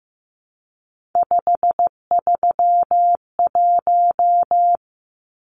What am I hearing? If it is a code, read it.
Morse code, heard as a keyed tone, about 15 wpm, 700 Hz. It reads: 531
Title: Numbers - 3-Digit 15wpm